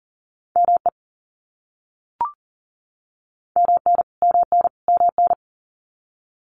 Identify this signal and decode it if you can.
Morse code, heard as a keyed tone, about 40 wpm, 700 Hz. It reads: ME MN MN MN
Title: US States Repeated 3x 40wpm